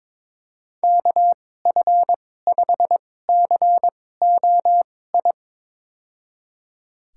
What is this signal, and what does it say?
Morse code, heard as a keyed tone, about 22 wpm, spaced as for 16 wpm, 700 Hz. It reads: KF5COI
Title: USA Call Signs - 6 Characters - 2x Character Spacing - 22wpm